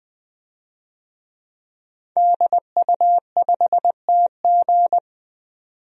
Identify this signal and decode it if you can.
Morse code, heard as a keyed tone, about 21 wpm, 700 Hz. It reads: DU5TG
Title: DXCC-100 Call Signs - 5 Characters - 20wpm